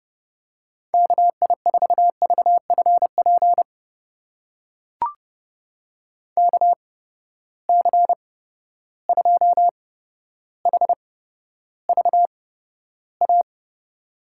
Morse code, heard as a keyed tone, about 30 wpm, spaced as for 13 wpm, 700 Hz. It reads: KI4VFP KC2HVA
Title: USA Call Signs - 6 Characters - 8x Character Spacing - 30wpm